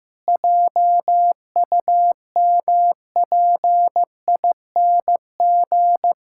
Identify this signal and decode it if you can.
Morse code, heard as a keyed tone, about 15 wpm, 700 Hz. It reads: JUMPING
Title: Pangram Challenge v2 - 15wpm